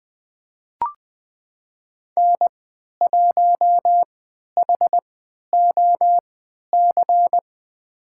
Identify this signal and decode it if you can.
Morse code, heard as a keyed tone, about 21 wpm, spaced as for 12 wpm, 700 Hz. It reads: N1HOC
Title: USA Call Signs - 5 Characters - 3x Character Spacing - 20wpm